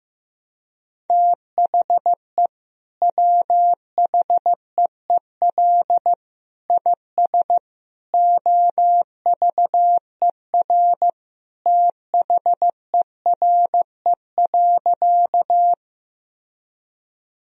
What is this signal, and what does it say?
Morse code, heard as a keyed tone, about 15 wpm, 700 Hz. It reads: THE WHEEL IS OVER THERE.
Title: Sentences from Top 1000 Words - Review Part 3 - 15wpm